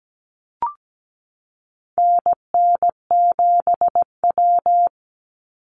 Morse code, heard as a keyed tone, about 17 wpm, 700 Hz. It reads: NN7W
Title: Call Signs Repeated 3x - 17wpm